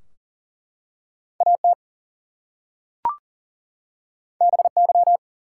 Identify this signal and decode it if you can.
Morse code, heard as a keyed tone, about 40 wpm, 700 Hz. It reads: AT BY
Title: Two Letter Words - 40wpm